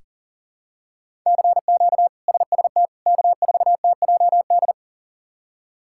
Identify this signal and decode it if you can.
Morse code, heard as a keyed tone, about 40 wpm, 700 Hz. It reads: CQ SST K4TJD
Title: K1USN SST QSO Practice - 40wpm